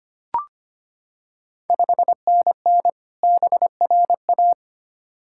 Morse code, heard as a keyed tone, about 25 wpm, 690 Hz. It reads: 5NN BRA
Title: Ontario QSO Party Exchange 25wpm